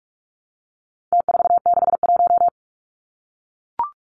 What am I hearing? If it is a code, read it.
Morse code, heard as a keyed tone, about 44 wpm, 700 Hz. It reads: T461
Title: Numbers - Contest Style - 4-Digit 45wpm